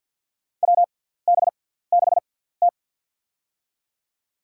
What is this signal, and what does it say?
Morse code, heard as a keyed tone, about 50 wpm, spaced as for 17 wpm, 700 Hz. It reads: WB6T
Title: USA Call Signs - 6x Character Spacing - 50wpm